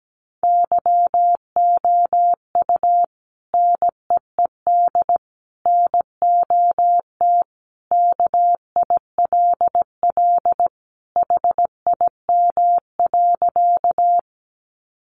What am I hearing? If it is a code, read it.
Morse code, heard as a keyed tone, about 17 wpm, 700 Hz. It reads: YOU NEED NOT KILL HIM.
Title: Sentences from Top 1400 Words 17wpm